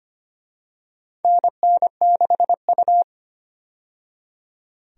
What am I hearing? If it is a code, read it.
Morse code, heard as a keyed tone, about 25 wpm, 700 Hz. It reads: NN6U
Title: USA Call Signs - 4 Characters - 25wpm